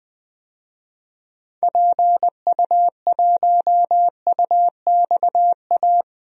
Morse code, heard as a keyed tone, about 20 wpm, 700 Hz. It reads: PU1UXA
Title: DXCC-100 Call Signs - 6 Characters - 20wpm